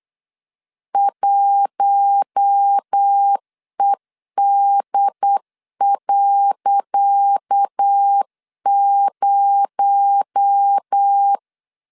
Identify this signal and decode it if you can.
Morse code, heard as a keyed tone, about 8 wpm, 800 Hz. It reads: 1ED.0
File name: LEZIONI / 10 SIGNS / 10C DOT 8wpm 3649